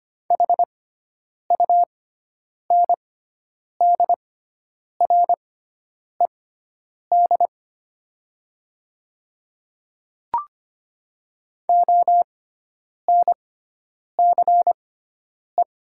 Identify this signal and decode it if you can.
Morse code, heard as a keyed tone, about 25 wpm, spaced as for 9 wpm, 700 Hz. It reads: HUNDRED ONCE
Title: 6x Character Spacing - Top 500 Words - 25wpm